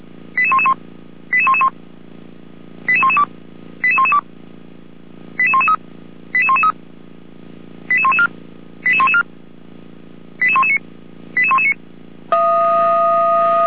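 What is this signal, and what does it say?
Five-tone selective call (ZVEI), heard as quick runs of five tones, tones of about 70 ms. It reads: 88111 88111 88112 88112 88113 88113 88114 88114 88119 88119
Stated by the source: ZVEI five-tone selective call 88129 received by SDR, from the Feuerwehr Orlamünde audio archive.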